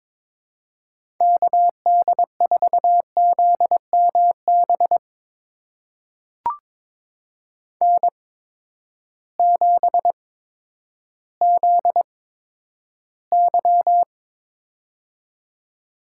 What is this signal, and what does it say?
Morse code, heard as a keyed tone, about 22 wpm, spaced as for 11 wpm, 700 Hz. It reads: KD4ZMB N7ZY
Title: USA Call Signs - 8x Character Spacing - 22wpm